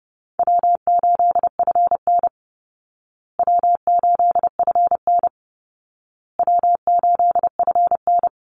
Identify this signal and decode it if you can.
Morse code, heard as a keyed tone, about 30 wpm, 700 Hz. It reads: W8FN W8FN W8FN
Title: Call Signs Repeated 2x - 30wpm